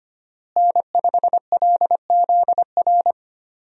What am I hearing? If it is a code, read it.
Morse code, heard as a keyed tone, about 25 wpm, 700 Hz. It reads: N5LZR
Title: USA Call Signs - 5 Characters - Repeated 3x - 25wpm